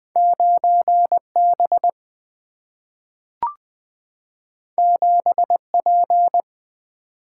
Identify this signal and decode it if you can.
Morse code, heard as a keyed tone, about 20 wpm, 700 Hz. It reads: 9 B 7 P
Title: Single-Letter Single-Number 20wpm